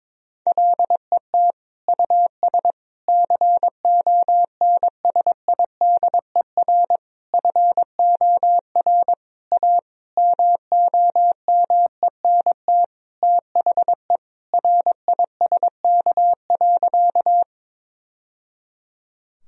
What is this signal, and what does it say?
Morse code, heard as a keyed tone, about 22 wpm, 700 Hz. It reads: LET US CONSIDER FOR A MOMENT THE RISK.
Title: Sentences from Top 1000 Words - Review Part 4 - 22wpm